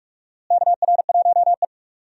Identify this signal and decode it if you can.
Morse code, heard as a keyed tone, about 42 wpm, 700 Hz. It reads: KR1E
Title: USA Call Signs - 4 Characters - Repeated 2x - 45wpm